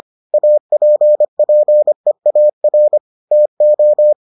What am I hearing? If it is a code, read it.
Morse code, heard as a keyed tone, about 25 wpm, 600 Hz. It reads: APPEAR TO